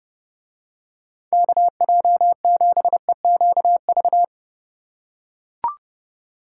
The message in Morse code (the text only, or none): KJ7EQV